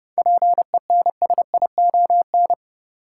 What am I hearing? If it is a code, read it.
Morse code, heard as a keyed tone, about 30 wpm, 700 Hz. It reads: PENSION
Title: Seven Letter Words - 30wpm